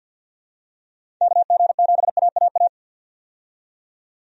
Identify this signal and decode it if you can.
Morse code, heard as a keyed tone, about 53 wpm, 700 Hz. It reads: KG7AAA
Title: USA Call Signs - 6 Characters - 50wpm